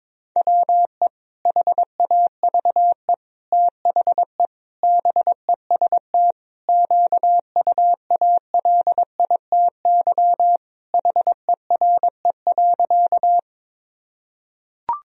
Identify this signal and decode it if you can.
Morse code, heard as a keyed tone, about 22 wpm, 700 Hz. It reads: WE HAVE THE BEST QUALITY HERE.
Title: Sentences from Top 1000 Words - Review Part 4 - 22wpm